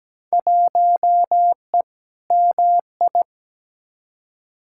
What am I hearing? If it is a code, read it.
Morse code, heard as a keyed tone, about 17 wpm, 700 Hz. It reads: 1E MI
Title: Field Day Exchanges - 17wpm